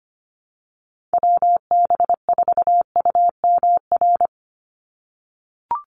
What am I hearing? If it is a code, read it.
Morse code, heard as a keyed tone, about 25 wpm, 700 Hz. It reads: WB4UMR
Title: USA Call Signs 25wpm